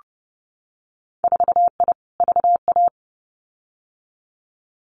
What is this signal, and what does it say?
Morse code, heard as a keyed tone, about 30 wpm, 700 Hz. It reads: 4I VA
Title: Winter Field Day Exchanges 30wpm